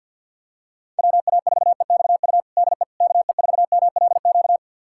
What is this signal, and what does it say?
Morse code, heard as a keyed tone, about 47 wpm, 700 Hz. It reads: WA4EXU DE KE4MLQ
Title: USA Callsign DE Callsign - 6 Characters - 50wpm